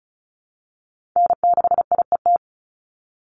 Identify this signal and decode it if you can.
Morse code, heard as a keyed tone, about 34 wpm, 700 Hz. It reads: N6IET
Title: Call Signs Repeated 2x - 35wpm